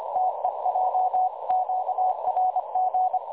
Morse code, heard as a keyed tone, about 25 wpm, 740 Hz, with noise. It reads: TEJ QRG
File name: dlazaawansowanych / nagraniazpasma / websdr recording 3565.0kHz